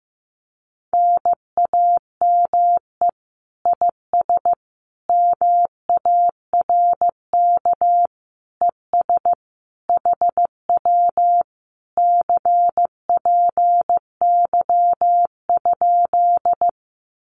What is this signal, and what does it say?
Morse code, heard as a keyed tone, about 15 wpm, 700 Hz. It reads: NAME IS MARK ES HW CPY?
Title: Common QSO Phrases Encore 15wpm